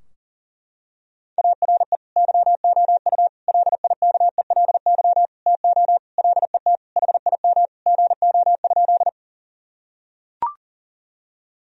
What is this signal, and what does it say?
Morse code, heard as a keyed tone, about 40 wpm, 700 Hz. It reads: ARE YOU LIKELY TO LET HIM GO?
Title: Sentences from Top 1000 Words - Review Part 4 - 40wpm